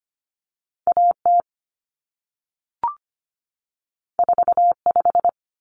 Morse code, heard as a keyed tone, about 25 wpm, 700 Hz. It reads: AT 45
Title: Common Cut Numbers - 2-Digit 25wpm